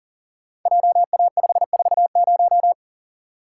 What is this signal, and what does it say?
Morse code, heard as a keyed tone, about 40 wpm, 700 Hz. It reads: JA540
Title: Code Groups - 5 Letter:Numbers - 40wpm